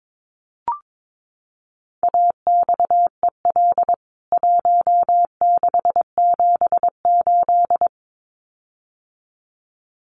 Morse code, heard as a keyed tone, about 22 wpm, 700 Hz. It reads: AXEL 1678
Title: CWT Exchange 22wpm